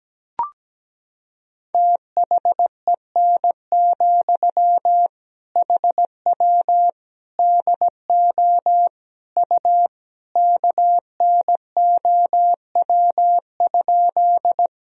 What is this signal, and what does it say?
Morse code, heard as a keyed tone, about 17 wpm, 700 Hz. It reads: THEN, HW DO U KNOW?
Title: Sentences from Top 100 Words and Abbreviations 17wpm